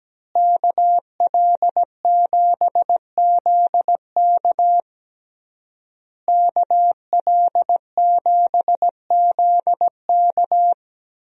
Morse code, heard as a keyed tone, about 17 wpm, 700 Hz. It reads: KL7ZK KL7ZK
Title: USA Call Signs - 5 Characters - Repeated 3x - 17wpm